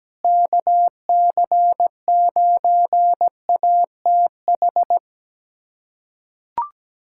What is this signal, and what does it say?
Morse code, heard as a keyed tone, about 17 wpm, 700 Hz. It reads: KC9ATH
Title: USA Callsign DE Callsign - 6 Characters - 17wpm